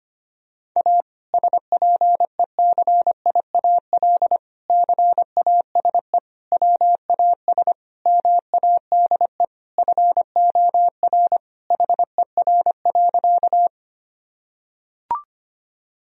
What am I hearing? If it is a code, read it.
Morse code, heard as a keyed tone, about 25 wpm, 700 Hz. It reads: A SPECIAL CASE WAS MADE FOR HER.
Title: Sentences from Top 1000 Words - Review Part 3 - 25wpm